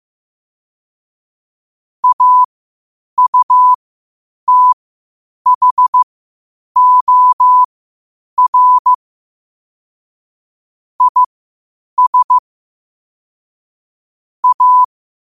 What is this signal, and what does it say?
Morse code, heard as a keyed tone, about 15 wpm, spaced as for 8 wpm, 1000 Hz. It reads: AUTHOR IS A
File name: qotdfiles / 8wpm